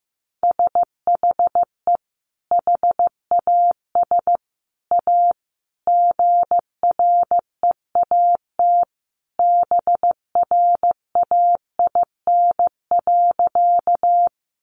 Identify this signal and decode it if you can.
Morse code, heard as a keyed tone, about 15 wpm, 700 Hz. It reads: SHE HAS A GREAT BRAIN.
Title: Sentences from Top 1100 Words 15wpm